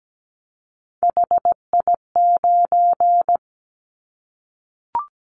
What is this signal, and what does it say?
Morse code, heard as a keyed tone, about 17 wpm, 700 Hz. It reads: HI9
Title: Callsign Prefixes for Top 100 DXCC - 17wpm